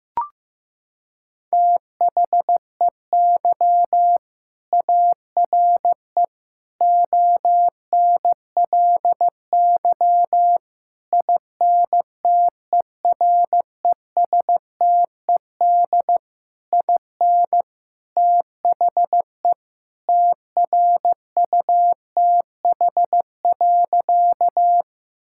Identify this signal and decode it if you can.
Morse code, heard as a keyed tone, about 16 wpm, 700 Hz. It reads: THEY ARE ONLY INTERESTED IN THE TRUTH.
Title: Sentences from Top 1000 Words - Review Part 6 - 15wpm